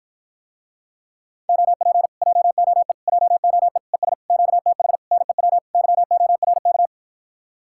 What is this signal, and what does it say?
Morse code, heard as a keyed tone, about 53 wpm, 700 Hz. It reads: OP JOE JOE ES QTH NEW YORK